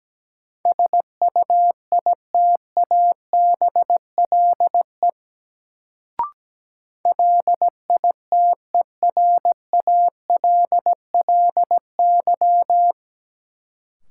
Morse code, heard as a keyed tone, about 17 wpm, 700 Hz. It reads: SUITABLE LITERALLY
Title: Sentences from Top 1400 Words - New Word Drill Alt - 17wpm